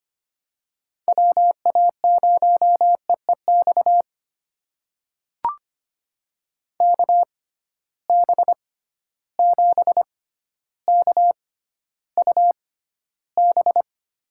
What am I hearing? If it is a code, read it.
Morse code, heard as a keyed tone, about 25 wpm, spaced as for 13 wpm, 700 Hz. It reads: WA0EEX KB7KUB
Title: USA Call Signs - 6 Characters - 6x Character Spacing - 25wpm